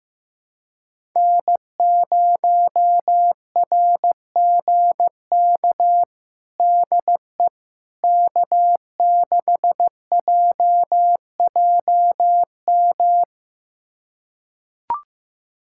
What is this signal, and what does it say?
Morse code, heard as a keyed tone, about 15 wpm, 700 Hz. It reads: N0RGK DE K6JJM
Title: USA Callsign DE Callsign - 5 Characters - 15wpm